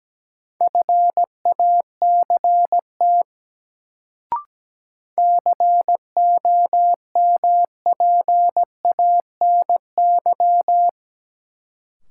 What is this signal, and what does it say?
Morse code, heard as a keyed tone, about 17 wpm, 700 Hz. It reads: FACT COMPANY
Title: Sets of 2 Words - New Word Drill - 17wpm